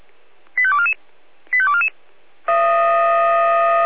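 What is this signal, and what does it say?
Five-tone selective call (ZVEI), heard as quick runs of five tones, tones of about 70 ms. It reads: 85244 85244
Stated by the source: ZVEI five-tone call 85217 received on SDR.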